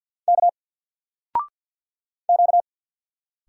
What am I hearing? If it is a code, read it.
Morse code, heard as a keyed tone, about 50 wpm, 700 Hz. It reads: K Q
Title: Single Letter-Number 50wpm